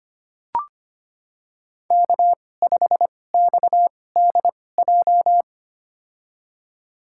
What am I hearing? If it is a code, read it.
Morse code, heard as a keyed tone, about 25 wpm, spaced as for 18 wpm, 700 Hz. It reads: K5XDJ
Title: USA Call Signs - 5 Characters - 2x Character Spacing - 25wpm